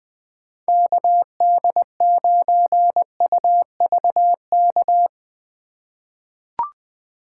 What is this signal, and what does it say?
Morse code, heard as a keyed tone, about 20 wpm, 700 Hz. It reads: KD9UVK
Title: 2023 POTA Hunter Call Signs - 20wpm